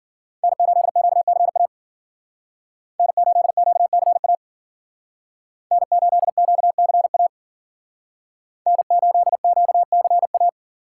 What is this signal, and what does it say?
Morse code, heard as a keyed tone, about 50 wpm, 700 Hz. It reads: N8QCA N8QCA N8QCA N8QCA